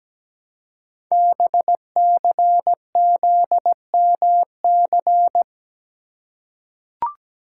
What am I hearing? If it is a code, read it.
Morse code, heard as a keyed tone, about 17 wpm, 700 Hz. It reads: BCZMC
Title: Code Groups - 5 Letters - 17wpm